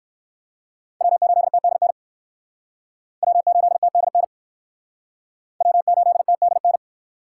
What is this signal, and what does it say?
Morse code, heard as a keyed tone, about 56 wpm, 700 Hz. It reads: W8TDN W8TDN W8TDN
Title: Speed Racing - USA 1x3 Call Signs 45wpm